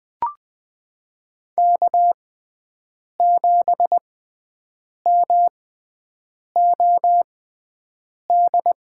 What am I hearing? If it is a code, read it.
Morse code, heard as a keyed tone, about 21 wpm, spaced as for 7 wpm, 700 Hz. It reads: K7MOD
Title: USA Call Signs - 5 Characters - 6x Character Spacing - 20wpm